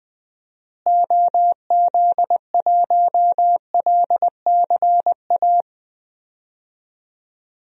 Morse code, heard as a keyed tone, about 20 wpm, 700 Hz. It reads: OZ1LCA
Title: DXCC-100 Call Signs - 6 Characters - Repeated 2x - 20wpm